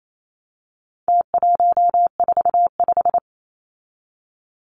Morse code, heard as a keyed tone, about 28 wpm, 700 Hz. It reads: T145
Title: Numbers - Contest Style - 4-Digit 28wpm